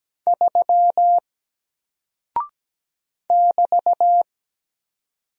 Morse code, heard as a keyed tone, about 17 wpm, 700 Hz. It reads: 3 <BT>
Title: Single Character (Letter:Number:Punctuation:Prosign) Alt 17wpm